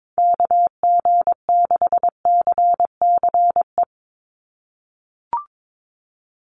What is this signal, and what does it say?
Morse code, heard as a keyed tone, about 22 wpm, 700 Hz. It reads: KG6CCE
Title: USA Call Signs 22wpm